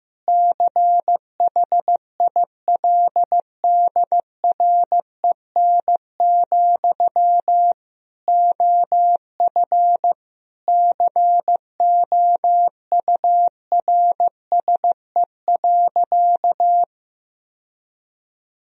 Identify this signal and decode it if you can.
Morse code, heard as a keyed tone, about 15 wpm, 700 Hz. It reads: CHILDREN, OF COURSE.
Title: Sentences from Top 1000 Words - Review Part 1 - 15wpm